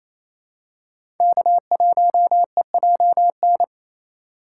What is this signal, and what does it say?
Morse code, heard as a keyed tone, about 28 wpm, 700 Hz. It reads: K1EJN
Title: USA Call Signs - 5 Characters - 4x Character Spacing - 28wpm